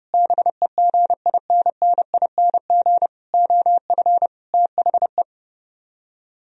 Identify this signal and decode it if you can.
Morse code, heard as a keyed tone, about 31 wpm, 700 Hz. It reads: BEGINNING OF THE